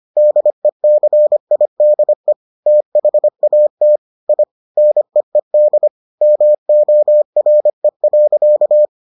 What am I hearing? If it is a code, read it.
Morse code, heard as a keyed tone, about 25 wpm, 600 Hz. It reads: DECIDE THAT I NEED MORE.